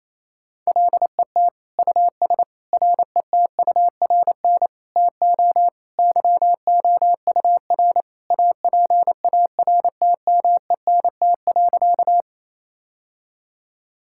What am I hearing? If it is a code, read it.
Morse code, heard as a keyed tone, about 27 wpm, 700 Hz. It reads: LET US RETURN TO YOUR APARTMENT.
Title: Sentences from Top 1000 Words - Review Part 6 - 28wpm